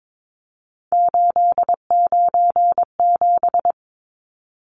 Morse code, heard as a keyed tone, about 22 wpm, 700 Hz. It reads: 897
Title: Numbers - 3-Digit 22wpm